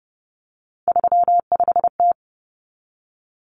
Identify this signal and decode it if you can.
Morse code, heard as a keyed tone, about 30 wpm, 700 Hz. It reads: 35T
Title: Common Cut Numbers - 3-Digit 30wpm